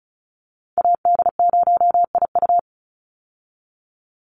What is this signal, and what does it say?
Morse code, heard as a keyed tone, about 34 wpm, 700 Hz. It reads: AD0IU
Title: Call Signs - 35wpm